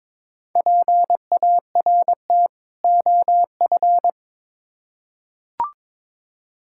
Morse code, heard as a keyed tone, about 22 wpm, 700 Hz. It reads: PART OF
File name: Build A Phrase - 2-Word Phrases - 22wpm